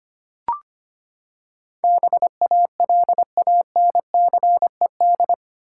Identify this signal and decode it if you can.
Morse code, heard as a keyed tone, about 25 wpm, 700 Hz. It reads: BALANCED
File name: Eight Letter Words - 25wpm